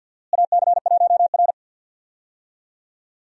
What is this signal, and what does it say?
Morse code, heard as a keyed tone, about 51 wpm, 700 Hz. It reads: AC1R
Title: USA Call Signs - 4 Characters - Repeated 3x - 50wpm